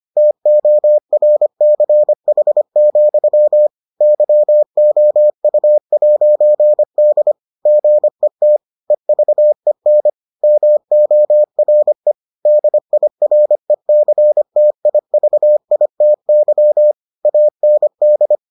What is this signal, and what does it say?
Morse code, heard as a keyed tone, about 25 wpm, 600 Hz. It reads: TORCH, YOU'D GET EVEN MORE DIRECTIVITY AND